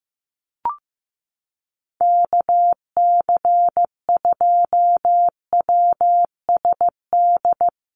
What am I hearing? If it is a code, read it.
Morse code, heard as a keyed tone, about 15 wpm, 700 Hz. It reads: KC2WSD
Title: USA Call Signs Repeated 3x 15wpm